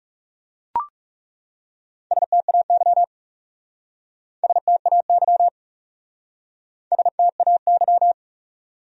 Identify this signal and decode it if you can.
Morse code, heard as a keyed tone, about 37 wpm, 700 Hz. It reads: STAY STAY STAY
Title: Speed-Racing Top 500 Words 30wpm